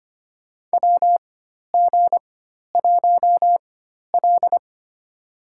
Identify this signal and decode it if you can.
Morse code, heard as a keyed tone, about 25 wpm, spaced as for 12 wpm, 700 Hz. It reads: WG1L
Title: USA Call Signs - 4 Characters - 4x Character Spacing - 25wpm